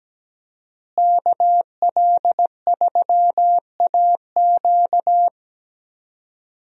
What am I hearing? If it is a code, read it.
Morse code, heard as a keyed tone, about 17 wpm, 700 Hz. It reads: KL3AQ
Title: USA Call Signs - 5 Characters - Repeated 3x - 17wpm